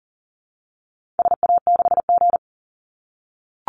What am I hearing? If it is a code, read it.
Morse code, heard as a keyed tone, about 40 wpm, 700 Hz. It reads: SA6G
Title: Call Signs - 40wpm